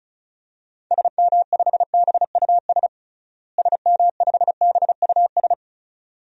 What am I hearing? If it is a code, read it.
Morse code, heard as a keyed tone, about 36 wpm, 700 Hz. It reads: SM5BUS SM5BUS
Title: DXCC-100 Call Signs - 6 Characters - Repeated 2x - 35wpm